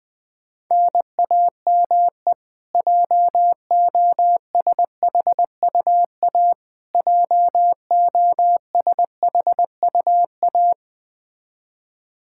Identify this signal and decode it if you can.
Morse code, heard as a keyed tone, about 20 wpm, 700 Hz. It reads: NAME JOSHUA JOSHUA
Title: Practice QSO - Exchange 1 - Beginning - 20wpm